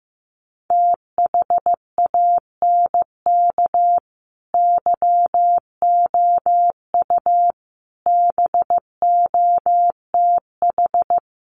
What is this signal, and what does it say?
Morse code, heard as a keyed tone, about 15 wpm, 700 Hz. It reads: THANK YOU BOTH
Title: Sets of 5 Words 15wpm